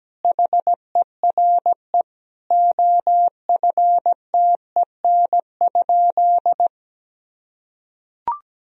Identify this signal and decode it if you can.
Morse code, heard as a keyed tone, about 17 wpm, 700 Hz. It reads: HERE OFTEN?
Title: Sentences from Top 1000 Words - Review Part 1 - 17wpm